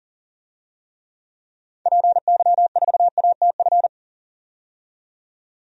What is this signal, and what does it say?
Morse code, heard as a keyed tone, about 40 wpm, 700 Hz. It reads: PY4ATF
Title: DXCC-100 Call Signs - 6 Characters - 40wpm